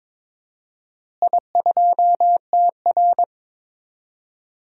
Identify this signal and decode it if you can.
Morse code, heard as a keyed tone, about 22 wpm, 700 Hz. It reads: I2TR
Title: DXCC-100 Call Signs - 4 Characters - Repeated 3x - 22wpm